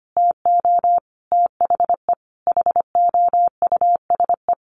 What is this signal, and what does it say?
Morse code, heard as a keyed tone, about 25 wpm, 700 Hz. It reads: TO THE HOUSE
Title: Sentences from Top 1500 Words 25wpm